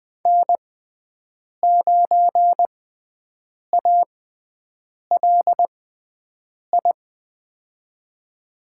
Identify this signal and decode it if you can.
Morse code, heard as a keyed tone, about 20 wpm, spaced as for 7 wpm, 700 Hz. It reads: N9ALI
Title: USA Call Signs - 5 Characters - 6x Character Spacing - 20wpm